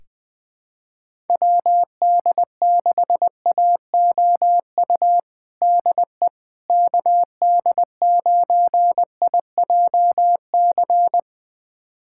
Morse code, heard as a keyed tone, about 21 wpm, 700 Hz. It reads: WD6AOU DE KD9IJC
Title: USA Callsign DE Callsign - 6 Characters - 20wpm